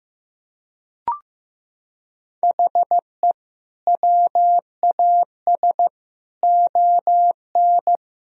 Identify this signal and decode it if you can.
Morse code, heard as a keyed tone, about 15 wpm, 700 Hz. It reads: HE WAS ON
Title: Sentences from Top 1000 Words - Review Part 3 - 15wpm